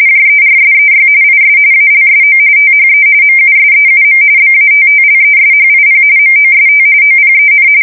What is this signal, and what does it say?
RTTY, about 45.45 baud. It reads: JERSEY IS ALSO THE MUSEUM BATTLESHIP MOST OPEN
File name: fsk170 / USS-New-Jersey